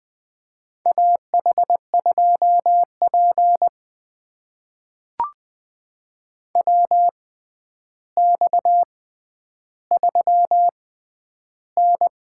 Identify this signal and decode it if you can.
Morse code, heard as a keyed tone, about 20 wpm, spaced as for 10 wpm, 700 Hz. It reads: AH2P WX3N
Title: USA Call Signs - 6x Character Spacing - 20wpm